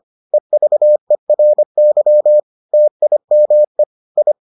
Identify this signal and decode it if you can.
Morse code, heard as a keyed tone, about 25 wpm, 600 Hz. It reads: EVERY TIME I